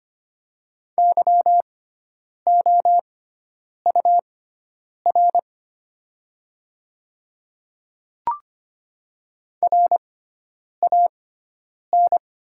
Morse code, heard as a keyed tone, about 25 wpm, spaced as for 9 wpm, 690 Hz. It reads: YOUR RAN
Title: 6x Character Spacing - Top 500 Words - 25wpm